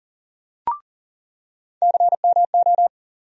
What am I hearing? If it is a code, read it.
Morse code, heard as a keyed tone, about 40 wpm, 700 Hz. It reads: CMO
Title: USA Call Sign Postfixes - 3 Letters 40wpm